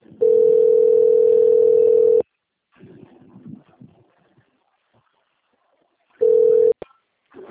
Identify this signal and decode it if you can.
Ringback tone (North America), 440 + 480 Hz.